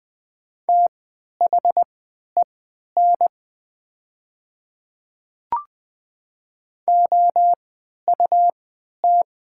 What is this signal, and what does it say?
Morse code, heard as a keyed tone, about 20 wpm, spaced as for 11 wpm, 700 Hz. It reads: THEN OUT
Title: 3x Character Spacing - Top 100 Words - 20wpm